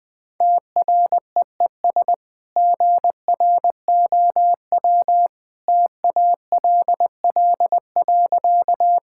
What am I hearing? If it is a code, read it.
Morse code, heard as a keyed tone, about 20 wpm, 700 Hz. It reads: TREES GROW TALL.
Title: Easy Sentences with Hints - 20wpm